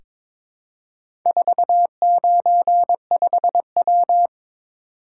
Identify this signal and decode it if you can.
Morse code, heard as a keyed tone, about 22 wpm, 700 Hz. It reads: 495W
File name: practice / 22wpm / Four Letter-Number Combinations - 22wpm